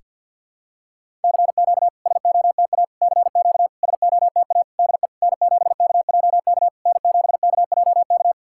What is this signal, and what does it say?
Morse code, heard as a keyed tone, about 51 wpm, 690 Hz. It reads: CQ SOTA CQ SOTA DE N7KJK N7KJK
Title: SOTA QSO Practice 50wpm